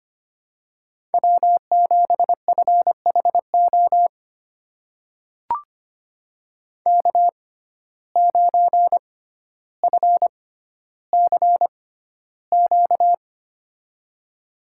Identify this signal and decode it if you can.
Morse code, heard as a keyed tone, about 25 wpm, spaced as for 13 wpm, 700 Hz. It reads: W7FHO K9FCQ
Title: USA Call Signs - 5 Characters - 6x Character Spacing - 25wpm